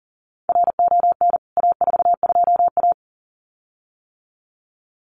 Morse code, heard as a keyed tone, about 40 wpm, 700 Hz. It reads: RON A42A
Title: CWT Exchange 40wpm